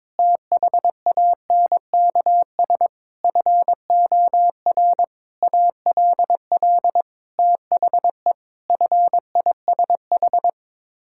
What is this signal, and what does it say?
Morse code, heard as a keyed tone, about 22 wpm, 700 Hz. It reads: THANKS FOR ALL THE FISH
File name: practice / 22wpm / Sets of 8 Words - 22wpm